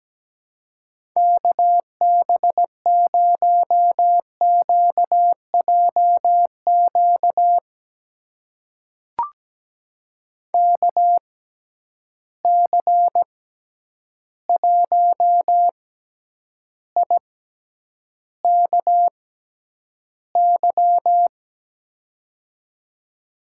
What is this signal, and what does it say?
Morse code, heard as a keyed tone, about 17 wpm, spaced as for 9 wpm, 700 Hz. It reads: KB0QJQ KC1IKY
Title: USA Call Signs - 6x Character Spacing - 17wpm